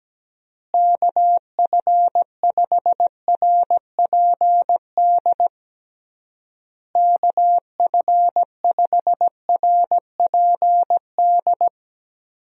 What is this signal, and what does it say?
Morse code, heard as a keyed tone, about 17 wpm, 700 Hz. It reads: KF5RPD KF5RPD
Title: USA Call Signs - 6 Characters - Repeated 3x - 17wpm